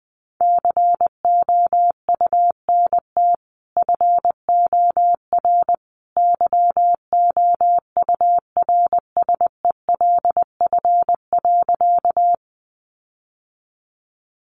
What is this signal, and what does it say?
Morse code, heard as a keyed tone, about 20 wpm, 700 Hz. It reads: COUNT FOR YOURSELF.
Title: Sentences from Top 1500 Words 20wpm